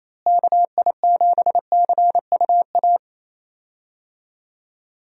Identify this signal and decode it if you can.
Morse code, heard as a keyed tone, about 27 wpm, 700 Hz. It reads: KI7CUA
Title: USA Call Signs - 6 Characters - Repeated 2x - 28wpm